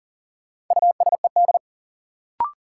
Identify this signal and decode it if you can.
Morse code, heard as a keyed tone, about 40 wpm, 700 Hz. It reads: USED
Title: Sentences from Top 600 Words - New Word Drill - 40wpm